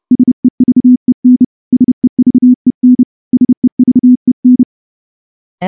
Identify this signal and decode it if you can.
Morse code, heard as a keyed tone, about 30 wpm, 250 Hz. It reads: SEVEN SEVEN SEVEN
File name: morse-training-30WPM